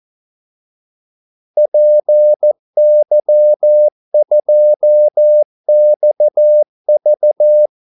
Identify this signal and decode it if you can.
Morse code, heard as a keyed tone, about 14 wpm, 600 Hz. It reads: PY2XV
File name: Callsigns-14wpm